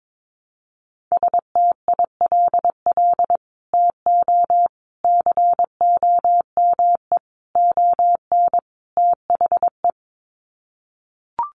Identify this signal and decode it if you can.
Morse code, heard as a keyed tone, about 22 wpm, 700 Hz. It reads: STILL TO COME ON THE
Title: Sets of 5 Words Encore2 22wpm